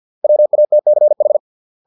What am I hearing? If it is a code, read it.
Morse code, heard as a keyed tone, about 50 wpm, 600 Hz. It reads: WATCH